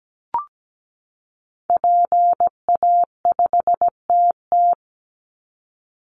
Morse code, heard as a keyed tone, about 17 wpm, 700 Hz. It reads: PA5TT
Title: Call Signs Repeated 2x - 17wpm